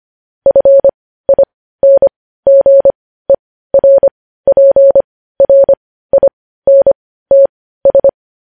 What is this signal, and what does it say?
Morse code, heard as a keyed tone, about 25 wpm, spaced as for 15 wpm, 550 Hz. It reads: FINGERPRINTS
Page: news-headlines-in-morse-code-at-15-wpm